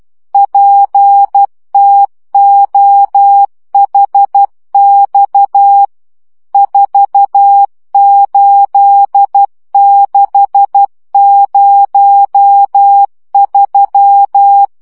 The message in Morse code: PTOHX 48603